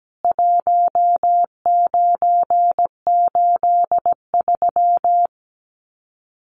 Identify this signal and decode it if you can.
Morse code, heard as a keyed tone, about 17 wpm, 700 Hz. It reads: 1983
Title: Common QSO Phrases 17wpm